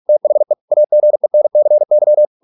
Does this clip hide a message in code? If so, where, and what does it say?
Yes: Morse code from 0.1 to 2.3 s, heard as a keyed tone, about 46 wpm, 600 Hz. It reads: THE AGENCY